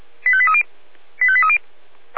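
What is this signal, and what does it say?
Five-tone selective call (ZVEI), heard as quick runs of five tones, tones of about 70 ms. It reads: 85830 85830
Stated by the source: ZVEI five-tone call received on SDR.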